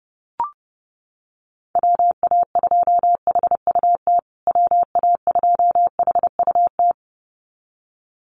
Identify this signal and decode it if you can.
Morse code, heard as a keyed tone, about 30 wpm, 700 Hz. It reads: WA2HUT WA2HUT
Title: USA Call Signs Repeated 2x 30wpm